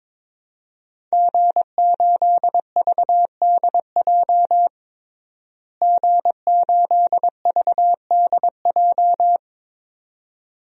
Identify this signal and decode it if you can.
Morse code, heard as a keyed tone, about 21 wpm, 700 Hz. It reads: G8VDJ G8VDJ
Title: DXCC-100 Call Signs - 5 Characters - Repeated 2x - 22wpm